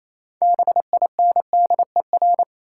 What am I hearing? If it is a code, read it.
Morse code, heard as a keyed tone, about 27 wpm, 700 Hz. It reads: BINDER
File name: Easy Words - Speed-Racing with Hints - 22wpm